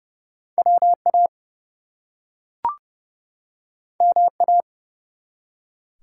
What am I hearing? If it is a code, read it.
Morse code, heard as a keyed tone, about 29 wpm, 700 Hz. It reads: WA MA
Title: US States and Canadian Provinces 30wpm